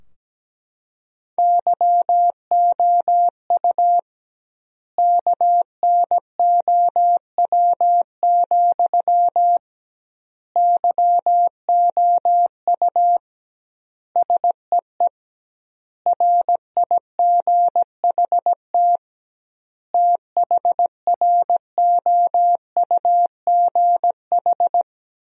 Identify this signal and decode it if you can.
Morse code, heard as a keyed tone, about 17 wpm, 700 Hz. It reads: YOU KNOW, YOU SEE RIGHT THROUGH
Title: 2x Word-Spacing - Sentences from Top 200 Words - 17wpm